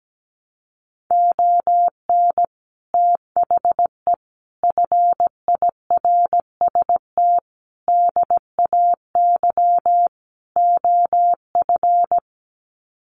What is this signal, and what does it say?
Morse code, heard as a keyed tone, about 17 wpm, 700 Hz. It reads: ON THE FIRST DAY OF
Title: Sets of 5 Words Encore 17wpm